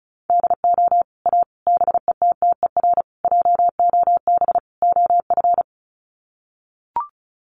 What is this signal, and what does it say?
Morse code, heard as a keyed tone, about 34 wpm, 700 Hz. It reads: DO A BETTER JOB OF
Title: Sets of 5 Words Encore2 35wpm